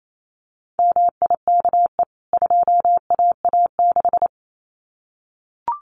Morse code, heard as a keyed tone, about 28 wpm, 700 Hz. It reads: MIKE 2AA6
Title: CWT Exchange 28wpm